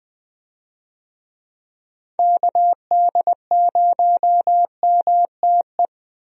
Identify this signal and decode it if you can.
Morse code, heard as a keyed tone, about 20 wpm, 700 Hz. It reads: KD0MTE
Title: USA Call Signs - 6 Characters - Repeated 2x - 20wpm